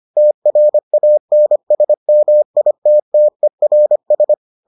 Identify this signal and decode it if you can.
Morse code, heard as a keyed tone, about 25 wpm, 600 Hz. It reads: TRANSMITTERS